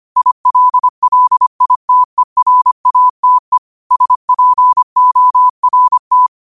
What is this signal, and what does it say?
Morse code, heard as a keyed tone, about 25 wpm, 1000 Hz. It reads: ILLITERATE SPORT
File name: qotdfiles / 25wpm